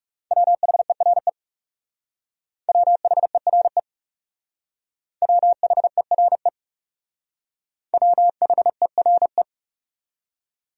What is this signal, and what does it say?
Morse code, heard as a keyed tone, about 39 wpm, 700 Hz. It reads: WHERE WHERE WHERE WHERE